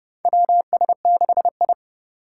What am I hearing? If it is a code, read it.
Morse code, heard as a keyed tone, about 30 wpm, 700 Hz. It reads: WS6I
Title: USA Call Signs - 4 Characters - 8x Character Spacing - 30wpm